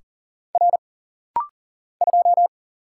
Morse code, heard as a keyed tone, about 39 wpm, 700 Hz. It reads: R 2
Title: Single Letter-Number - Rapid-Fire 40wpm